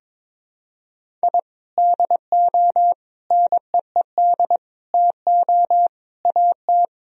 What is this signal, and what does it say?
Morse code, heard as a keyed tone, about 22 wpm, 700 Hz. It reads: I DO NEED TO AT
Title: Sentences from Top 1000 Words - Review Part 1 - 22wpm